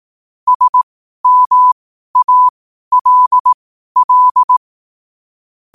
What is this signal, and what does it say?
Morse code, heard as a keyed tone, about 18 wpm, spaced as for 12 wpm, 1000 Hz. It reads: SMALL
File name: qotdfiles / 12wpm